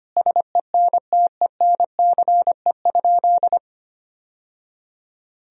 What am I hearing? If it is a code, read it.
Morse code, heard as a keyed tone, about 25 wpm, 700 Hz. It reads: SENTENCE?
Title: Sentences from Top 1000 Words - Review Part 1 - 25wpm